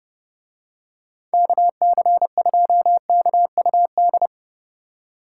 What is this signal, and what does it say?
Morse code, heard as a keyed tone, about 31 wpm, 700 Hz. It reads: KC2KUD